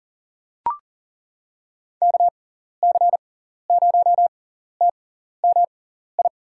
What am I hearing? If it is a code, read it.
Morse code, heard as a keyed tone, about 40 wpm, spaced as for 14 wpm, 700 Hz. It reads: KC0TMI
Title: USA Call Signs - 6 Characters - 6x Character Spacing - 40wpm